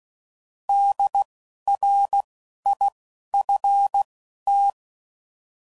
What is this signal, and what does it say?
Morse code, heard as a keyed tone, about 16 wpm, spaced as for 11 wpm, 800 Hz. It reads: DRIFT
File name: Prefixes / By First Letter / Ds